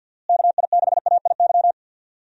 Morse code, heard as a keyed tone, about 51 wpm, 700 Hz. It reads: KI6AIY
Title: USA Call Signs - 6 Characters - Repeated 3x - 50wpm